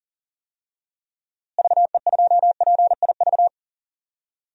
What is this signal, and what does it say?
Morse code, heard as a keyed tone, about 40 wpm, 700 Hz. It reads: VE2PIV
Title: DXCC-100 Call Signs - 6 Characters - Repeated 3x - 40wpm